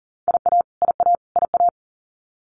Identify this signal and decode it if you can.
Morse code, heard as a keyed tone, about 40 wpm, 700 Hz. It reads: IA IA IA
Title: US States Repeated 3x 40wpm